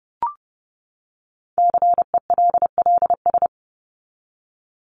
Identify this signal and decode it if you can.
Morse code, heard as a keyed tone, about 30 wpm, 700 Hz. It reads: CELLS
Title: ES:S Suffix - Top 100 Words with Base Words 30wpm